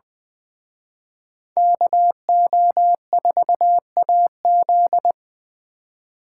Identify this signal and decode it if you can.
Morse code, heard as a keyed tone, about 24 wpm, 700 Hz. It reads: KO4AZ
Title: Speed Racing - USA Call Signs 20wpm